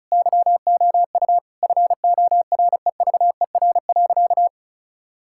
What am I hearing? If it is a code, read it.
Morse code, heard as a keyed tone, about 36 wpm, 700 Hz. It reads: YOU FOREVER.
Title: Sentences from Top 1600 Words 35wpm